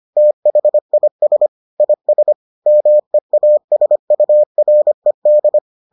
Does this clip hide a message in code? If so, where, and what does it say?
Yes: Morse code from 0.2 to 5.6 s, heard as a keyed tone, about 25 wpm, 600 Hz. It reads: THIS IS MEASURED